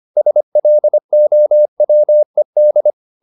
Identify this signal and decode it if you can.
Morse code, heard as a keyed tone, about 25 wpm, 600 Hz. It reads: SLOWED
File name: the old man and the sea part 1 25wpm